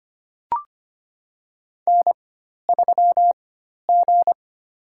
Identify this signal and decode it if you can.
Morse code, heard as a keyed tone, about 25 wpm, spaced as for 12 wpm, 690 Hz. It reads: N3G
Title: USA Call Signs - 4x Character Spacing - 25wpm